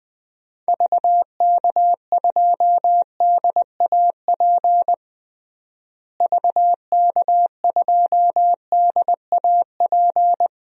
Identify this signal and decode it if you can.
Morse code, heard as a keyed tone, about 20 wpm, 700 Hz. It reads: VK2DAP VK2DAP
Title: DXCC-100 Call Signs - 6 Characters - Repeated 3x - 20wpm